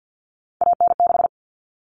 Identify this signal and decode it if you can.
Morse code, heard as a keyed tone, about 50 wpm, 700 Hz. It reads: AN6
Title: Common Cut Numbers - 3-Digit 50wpm